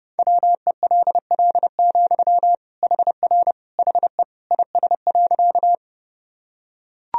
Morse code, heard as a keyed tone, about 32 wpm, 700 Hz. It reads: WELL, HR HE IS.
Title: Sentences from Top 200 Words and Abbreviations 30wpm